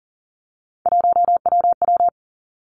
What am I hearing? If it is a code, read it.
Morse code, heard as a keyed tone, about 41 wpm, 700 Hz. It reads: 1WW
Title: Three Challenging Letter-Number Combinations 40wpm